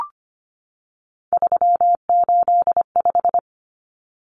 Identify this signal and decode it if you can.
Morse code, heard as a keyed tone, about 25 wpm, 700 Hz. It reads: 385
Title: Common Cut Numbers - 3-Digit 25wpm